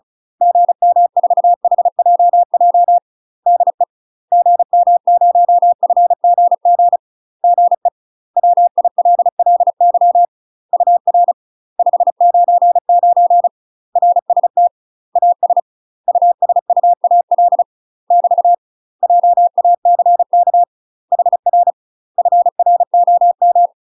Morse code, heard as a keyed tone, about 34 wpm, 700 Hz. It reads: GM4HJJ DE GM0FGG GE WILLY UR 599 RST AS USUAL <BT> JACK HR FROM
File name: UK MASTER QSOs TRAINING/35WPM